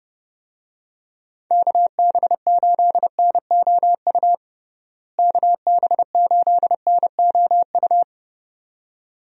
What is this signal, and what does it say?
Morse code, heard as a keyed tone, about 30 wpm, 700 Hz. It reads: KB8NOU KB8NOU
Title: USA Call Signs - 6 Characters - Repeated 2x - 30wpm